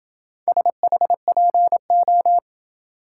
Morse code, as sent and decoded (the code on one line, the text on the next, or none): ... .... .--. ---
SHPO